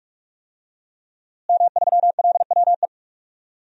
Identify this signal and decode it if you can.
Morse code, heard as a keyed tone, about 46 wpm, 700 Hz. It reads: M3LWE
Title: DXCC-100 Call Signs - 5 Characters - Repeated 2x - 45wpm